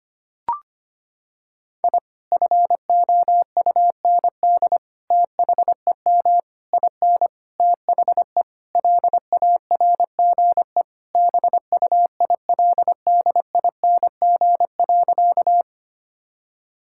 Morse code, heard as a keyed tone, about 25 wpm, 700 Hz. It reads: I FOUND THEM IN THE LARGE BUILDING.
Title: Sentences from Top 1000 Words - Review Part 5 - 25wpm